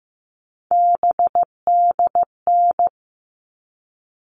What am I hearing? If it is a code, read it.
Morse code, heard as a keyed tone, about 15 wpm, 700 Hz. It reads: BDN
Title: All Cut Numbers - 3-Digit 15wpm